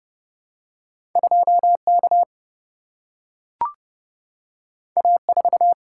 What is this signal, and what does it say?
Morse code, heard as a keyed tone, about 29 wpm, 700 Hz. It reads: 2K A4
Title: Single-Letter Single-Number 30wpm